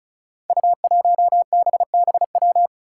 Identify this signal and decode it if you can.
Morse code, heard as a keyed tone, about 36 wpm, 700 Hz. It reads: U1BBW
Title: Five Challenging Letter-Number Combinations 35wpm